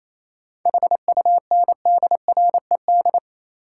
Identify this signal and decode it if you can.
Morse code, heard as a keyed tone, about 28 wpm, 700 Hz. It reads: HUNDRED
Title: Sentences from Top 400 Words - New Word Drill - 28wpm